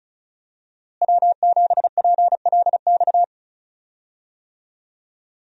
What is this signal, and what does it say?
Morse code, heard as a keyed tone, about 36 wpm, 700 Hz. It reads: W7PLX
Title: USA Call Signs - 5 Characters - Repeated 3x - 35wpm